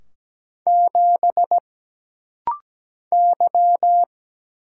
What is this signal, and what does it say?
Morse code, heard as a keyed tone, about 17 wpm, 700 Hz. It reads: 7 Y
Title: Single Letter-Number Alt - Rapid-Fire 17wpm